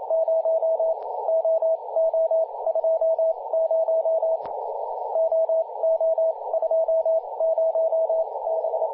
Morse code, heard as a keyed tone, about 27 wpm, 650 Hz, with radio noise. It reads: 0 O O 2 0 O O 2 0 B
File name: posts / 2016-july-17-0522z-jt1co-50-098mhz-cw